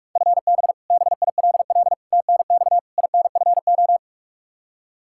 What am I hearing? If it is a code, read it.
Morse code, heard as a keyed tone, about 46 wpm, 700 Hz. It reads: FB BILL TNX INFO
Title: Practice QSO - Exchange 3 - More Info - 45wpm